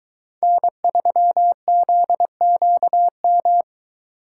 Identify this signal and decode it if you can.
Morse code, heard as a keyed tone, about 22 wpm, 700 Hz. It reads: N3ZQM